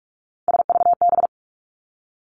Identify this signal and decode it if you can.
Morse code, heard as a keyed tone, about 46 wpm, 700 Hz. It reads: SVB
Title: Three Challenging Letter-Number Combinations 45wpm